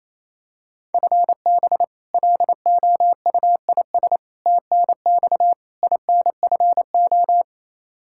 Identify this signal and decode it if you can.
Morse code, heard as a keyed tone, about 28 wpm, 700 Hz. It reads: FB LOUIS TNX INFO
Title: K1USN SST QSO Practice 28wpm